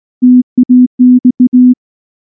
Morse code, heard as a keyed tone, about 19 wpm, 250 Hz. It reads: TAX
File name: morse / morse-training-20WPM-9